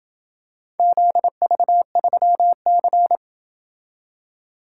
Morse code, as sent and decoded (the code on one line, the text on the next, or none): --.. ...- ...-- -.-.
ZV3C